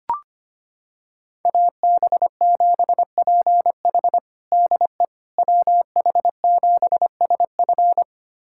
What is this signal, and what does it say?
Morse code, heard as a keyed tone, about 25 wpm, 700 Hz. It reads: AB7PH DE WH7SF
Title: USA Callsign DE Callsign - 5 Characters - 25wpm